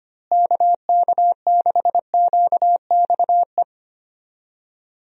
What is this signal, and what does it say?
Morse code, heard as a keyed tone, about 25 wpm, 700 Hz. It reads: KK6QXE
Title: USA Call Signs - 6 Characters - Repeated 2x - 25wpm